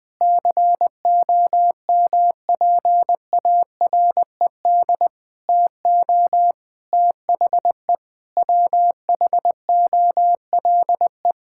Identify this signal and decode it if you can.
Morse code, heard as a keyed tone, about 20 wpm, 700 Hz. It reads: COMPARED TO THE WHOLE
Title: Sentences with 7-Word Phrases - 20wpm